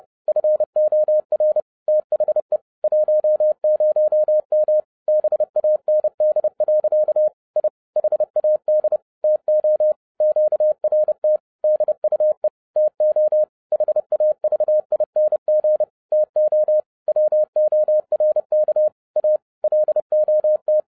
Morse code, heard as a keyed tone, about 30 wpm, 610 Hz. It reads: FOR THE 10M BAND. I HAD TO QRT DUE TO HAVING TO WORK A LOT
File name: 30wpm